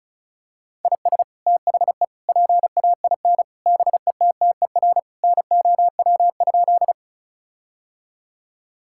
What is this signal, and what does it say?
Morse code, heard as a keyed tone, about 36 wpm, 700 Hz. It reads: IS THE PAIN BETTER NOW?
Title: Sentences from Top 1600 Words 35wpm